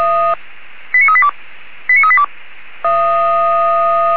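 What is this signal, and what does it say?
Five-tone selective call (ZVEI), heard as quick runs of five tones, tones of about 70 ms. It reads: 89382 89382
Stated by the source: ZVEI five-tone selective call received via SDR.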